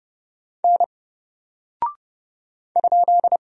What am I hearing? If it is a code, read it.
Morse code, heard as a keyed tone, about 30 wpm, 700 Hz. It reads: N ?
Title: Single Character (Letter:Number:Punctuation:Prosign) Alt 30wpm